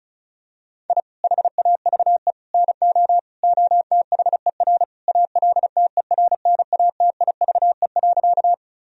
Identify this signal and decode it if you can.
Morse code, heard as a keyed tone, about 36 wpm, 700 Hz. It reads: I HAVE NO OTHER ALTERNATIVE.
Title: Sentences from Top 1000 Words - Review Part 4 - 35wpm